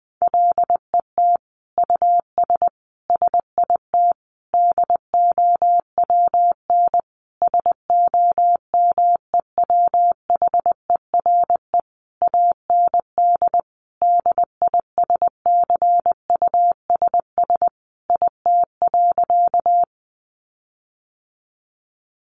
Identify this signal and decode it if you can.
Morse code, heard as a keyed tone, about 20 wpm, 700 Hz. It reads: LET US SIT DOWN SOMEWHERE AND DISCUSS IT.
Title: Sentences from Top 1400 Words 20wpm